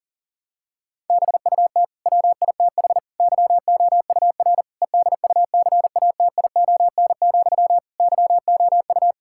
Morse code, heard as a keyed tone, about 40 wpm, 700 Hz. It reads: BUT WITH YOUR EDUCATION, YOU
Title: Sentences from Top 1000 Words - Review Part 6 - 40wpm